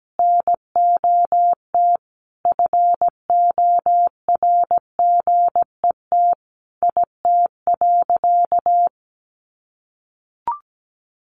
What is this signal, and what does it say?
Morse code, heard as a keyed tone, about 17 wpm, 700 Hz. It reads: NOT FORGET IT.
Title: Sentences from Top 1200 Words 17wpm